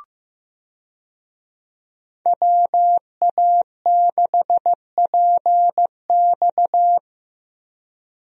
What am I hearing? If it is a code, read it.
Morse code, heard as a keyed tone, about 15 wpm, 700 Hz. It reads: WA6PX
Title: USA Call Signs - 5 Characters - Repeated 3x - 15wpm